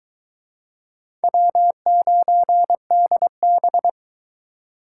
Morse code, heard as a keyed tone, about 24 wpm, 700 Hz. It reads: W9DB